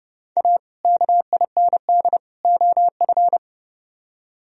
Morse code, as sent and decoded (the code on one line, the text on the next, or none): .- / -.- .. -. -.. / --- ..-.
A KIND OF